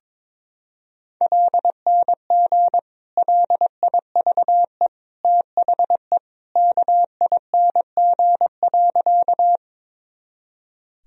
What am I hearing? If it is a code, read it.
Morse code, heard as a keyed tone, about 22 wpm, 700 Hz. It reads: LNG LIVE THE KING.
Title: Sentences from Top 400 Words and Abbreviations 22wpm